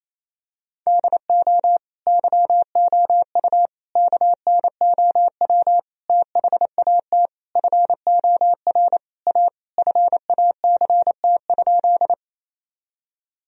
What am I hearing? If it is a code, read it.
Morse code, heard as a keyed tone, about 28 wpm, 700 Hz. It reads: DO YOU KNOW THAT FOR A FACT?
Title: Sentences from Top 1000 Words - Review Part 3 - 28wpm